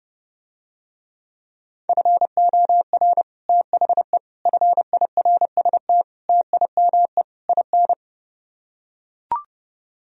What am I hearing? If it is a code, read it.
Morse code, heard as a keyed tone, about 32 wpm, 700 Hz. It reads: FOR THE FIRST TIME IN